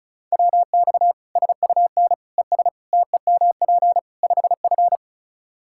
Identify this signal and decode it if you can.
Morse code, heard as a keyed tone, about 34 wpm, 700 Hz. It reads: WX SUN ES TEMP 5F
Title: Practice QSO - Exchange 2 - Info - 35wpm